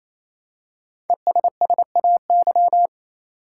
Morse code, heard as a keyed tone, about 28 wpm, 700 Hz. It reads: ESSAY
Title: Five Letter Words - 28wpm